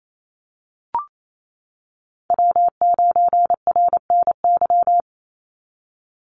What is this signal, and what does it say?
Morse code, heard as a keyed tone, about 28 wpm, 700 Hz. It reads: W9RNY
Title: Call Signs - 28wpm